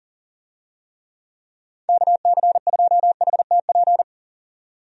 Morse code, heard as a keyed tone, about 40 wpm, 700 Hz. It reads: KC2HTP
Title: USA Call Signs - 6 Characters - 40wpm